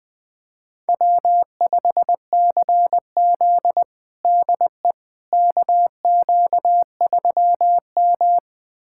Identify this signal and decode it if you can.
Morse code, heard as a keyed tone, about 20 wpm, 700 Hz. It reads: W5CZ DE KQ3M
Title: USA Callsign DE Callsign - 4 Characters - 20wpm